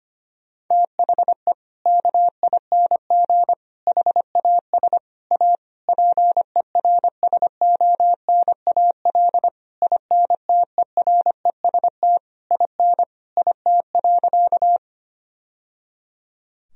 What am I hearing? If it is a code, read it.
Morse code, heard as a keyed tone, about 25 wpm, 700 Hz. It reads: THE KING HAS A PERSONAL INTEREST IN IT.
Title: Sentences from Top 1000 Words - Review Part 4 - 25wpm